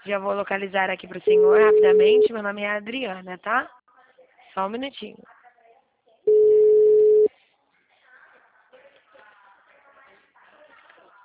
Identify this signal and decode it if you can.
Ringback tone (Europe, 425 Hz), with speech.